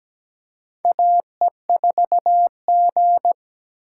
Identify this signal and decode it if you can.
Morse code, heard as a keyed tone, about 17 wpm, 700 Hz. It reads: AE4G
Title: USA Call Signs - 4 Characters - 3x Character Spacing - 17wpm